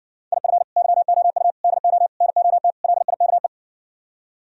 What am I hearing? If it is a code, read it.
Morse code, heard as a keyed tone, about 60 wpm, 700 Hz. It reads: IF YOU DO NOT LIKE